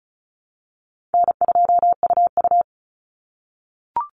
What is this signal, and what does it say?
Morse code, heard as a keyed tone, about 34 wpm, 700 Hz. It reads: N2UU
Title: Call Signs - 35wpm